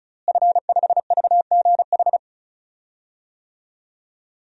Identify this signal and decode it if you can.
Morse code, heard as a keyed tone, about 36 wpm, 700 Hz. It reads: F5VGH
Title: DXCC-100 Call Signs - 5 Characters - Repeated 3x - 35wpm